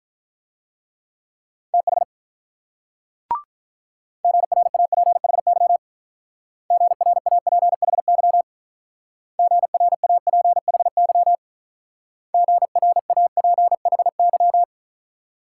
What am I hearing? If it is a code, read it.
Morse code, heard as a keyed tone, about 45 wpm, 700 Hz. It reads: TH GRAPHY GRAPHY GRAPHY GRAPHY